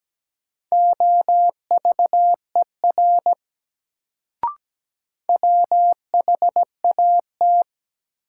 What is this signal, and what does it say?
Morse code, heard as a keyed tone, about 17 wpm, 700 Hz. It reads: OVER WHAT
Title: Sentences from Top 100 Words - New Word Drill - 17wpm